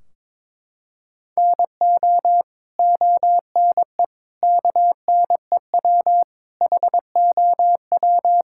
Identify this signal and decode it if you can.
Morse code, heard as a keyed tone, about 22 wpm, 700 Hz. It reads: NO ONE KNEW HOW
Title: Sentences from Top 1600 Words 22wpm